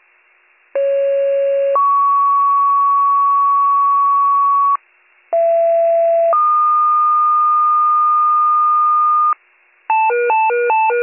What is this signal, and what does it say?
Two-tone page, heard as a one-second tone, then a three-second tone, about 550 then 1120 Hz, about 670 then 1220 Hz. It is heard from 0.7 to 9.3 s.